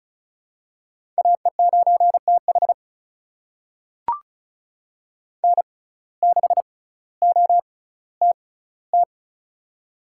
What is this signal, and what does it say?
Morse code, heard as a keyed tone, about 36 wpm, spaced as for 18 wpm, 700 Hz. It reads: AE9TH N6OTT
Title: USA Call Signs - 5 Characters - 6x Character Spacing - 35wpm